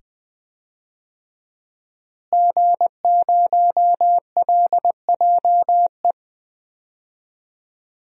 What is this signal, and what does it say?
Morse code, heard as a keyed tone, about 20 wpm, 700 Hz. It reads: G0LJE
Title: DXCC-100 Call Signs - 5 Characters - Repeated 2x - 20wpm